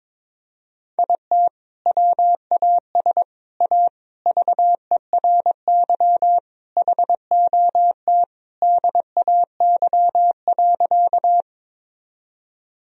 Morse code, heard as a keyed tone, about 22 wpm, 700 Hz. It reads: IT WAS A VERY HOT DAY.
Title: Sentences from Top 1000 Words - Review Part 3 - 22wpm